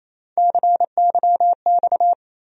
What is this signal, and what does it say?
Morse code, heard as a keyed tone, about 28 wpm, 700 Hz. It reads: CYX
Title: USA Call Sign Postfixes - 3 Letters 28wpm